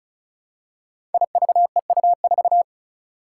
Morse code, heard as a keyed tone, about 36 wpm, 700 Hz. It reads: IVEU4
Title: Five Challenging Letter-Number Combinations 35wpm